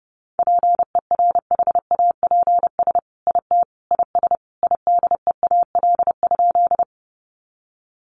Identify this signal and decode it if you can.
Morse code, heard as a keyed tone, about 30 wpm, 700 Hz. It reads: PERHAPS IT IS IDEAL?
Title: Sentences from Top 1400 Words 30wpm